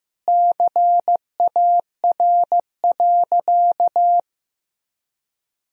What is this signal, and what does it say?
Morse code, heard as a keyed tone, about 16 wpm, 700 Hz. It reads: CAR.
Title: Sentences from Top 1000 Words - Review Part 1 - 15wpm